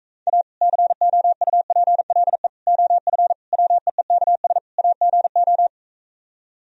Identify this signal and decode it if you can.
Morse code, heard as a keyed tone, about 44 wpm, 700 Hz. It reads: A COUPLE OF WEEKS AGO